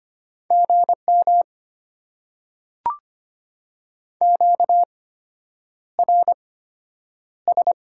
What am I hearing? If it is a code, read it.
Morse code, heard as a keyed tone, about 25 wpm, spaced as for 9 wpm, 690 Hz. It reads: GM QRS
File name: Top 90 QSO Elements - 8x Character Spacing - 25wpm